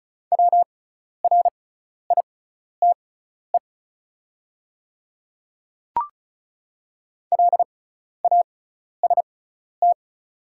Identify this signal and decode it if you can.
Morse code, heard as a keyed tone, about 34 wpm, spaced as for 12 wpm, 700 Hz. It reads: WRITE LAST
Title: 6x Character Spacing - Top 300 Words - 35wpm